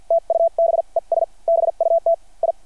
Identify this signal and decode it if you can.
Morse code, heard as a keyed tone, about 49 wpm, spaced as for 40 wpm, 600 Hz. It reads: TUBES BUT I
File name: archived / CRSnet